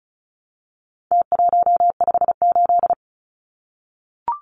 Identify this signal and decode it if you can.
Morse code, heard as a keyed tone, about 34 wpm, 700 Hz. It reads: T158
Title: Numbers - Contest Style - 4-Digit 35wpm